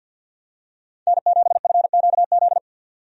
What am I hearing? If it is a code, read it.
Morse code, heard as a keyed tone, about 50 wpm, 700 Hz. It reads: N7FQZ
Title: USA Call Signs - 5 Characters - 50wpm